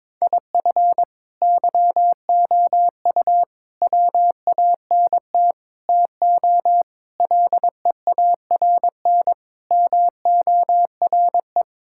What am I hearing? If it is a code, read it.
Morse code, heard as a keyed tone, about 22 wpm, 700 Hz. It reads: IF YOU WANT TO LEARN MORE
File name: Sentences with 7-Word Phrases - 22wpm